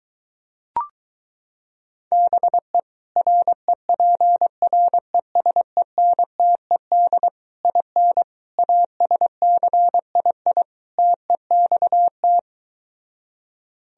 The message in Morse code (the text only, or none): BE REPRESENTED IN ASCII TEXT